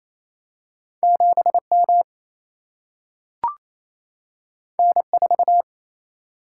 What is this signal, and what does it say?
Morse code, heard as a keyed tone, about 28 wpm, 700 Hz. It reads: 7 M N 4
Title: Single-Letter Single-Number 28wpm